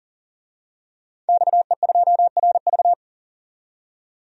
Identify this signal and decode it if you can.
Morse code, heard as a keyed tone, about 40 wpm, 700 Hz. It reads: XE2RV
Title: DXCC-100 Call Signs - 5 Characters - 40wpm